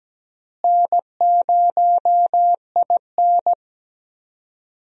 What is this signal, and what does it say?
Morse code, heard as a keyed tone, about 17 wpm, 700 Hz. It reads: N0IN
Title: USA Call Signs - 4 Characters - Repeated 2x - 17wpm